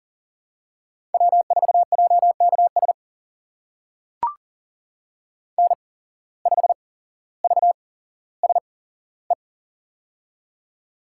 Morse code, heard as a keyed tone, about 38 wpm, spaced as for 17 wpm, 700 Hz. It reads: W4JKS N5VSE
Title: USA Call Signs - 5 Characters - 8x Character Spacing - 40wpm